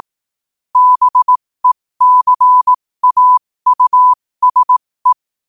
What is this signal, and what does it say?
Morse code, heard as a keyed tone, about 18 wpm, spaced as for 15 wpm, 1000 Hz. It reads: BECAUSE
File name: qotdfiles / 15wpm